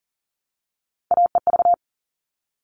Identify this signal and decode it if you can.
Morse code, heard as a keyed tone, about 40 wpm, 700 Hz. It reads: AEV
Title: All Cut Numbers - 3-Digit 40wpm